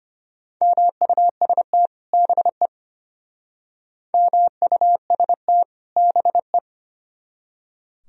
Morse code, heard as a keyed tone, about 34 wpm, 700 Hz. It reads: MUST BE MUST BE